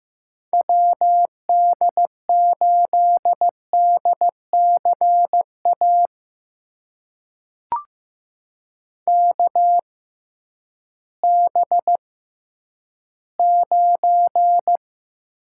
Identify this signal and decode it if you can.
Morse code, heard as a keyed tone, about 15 wpm, spaced as for 10 wpm, 700 Hz. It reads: WD8DCA KB9
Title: USA Call Signs - 6 Characters - 6x Character Spacing - 15wpm